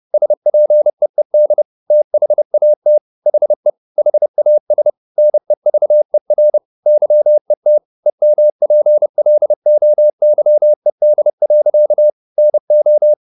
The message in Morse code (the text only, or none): SPEED THAT HE HAS NEVER YET EMPLOYED. NO